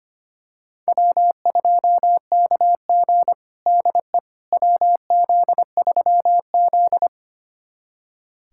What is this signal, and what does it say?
Morse code, heard as a keyed tone, about 25 wpm, 700 Hz. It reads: W2KG DE WZ3Z
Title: USA Callsign DE Callsign - 4 Characters - 25wpm